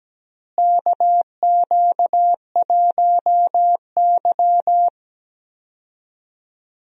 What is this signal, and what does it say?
Morse code, heard as a keyed tone, about 17 wpm, 700 Hz. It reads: KQ1Y
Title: USA Call Signs - 4 Characters - Repeated 2x - 17wpm